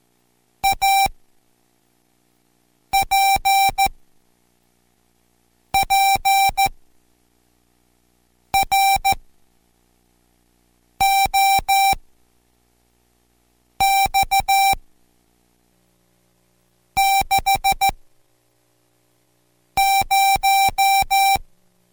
Morse code, heard as a keyed tone, about 14 wpm, spaced as for 4 wpm, 780 Hz. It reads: APPROX 60